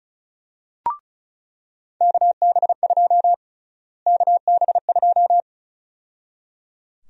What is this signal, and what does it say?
Morse code, heard as a keyed tone, about 34 wpm, 700 Hz. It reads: KB2 KB2
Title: USA Call Sign Prefixes - Repeated 2x - 35wpm